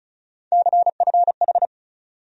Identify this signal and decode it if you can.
Morse code, heard as a keyed tone, about 36 wpm, 700 Hz. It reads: CFH
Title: USA Call Sign Postfixes - 3 Letters 35wpm